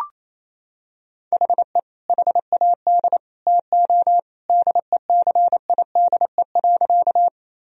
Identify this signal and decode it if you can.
Morse code, heard as a keyed tone, about 28 wpm, 700 Hz. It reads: HE HAD TO DECIDE.
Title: Sentences from Top 1000 Words - Review Part 3 - 28wpm